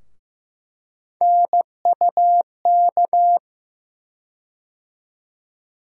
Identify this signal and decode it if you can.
Morse code, heard as a keyed tone, about 15 wpm, 700 Hz. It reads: NUK
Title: USA Call Sign Postfixes - 3 Letters 15wpm